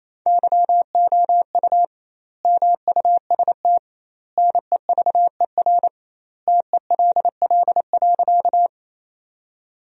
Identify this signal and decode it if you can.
Morse code, heard as a keyed tone, about 27 wpm, 700 Hz. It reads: YOU MUST NEVER TELL.
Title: 2x Word-Spacing - Sentences from Top 300 Words - 28wpm